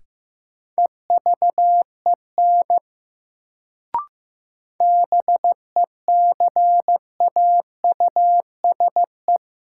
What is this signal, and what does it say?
Morse code, heard as a keyed tone, about 15 wpm, 700 Hz. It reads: EVEN BECAUSE
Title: Sentences from Top 600 Words - New Word Drill - 15wpm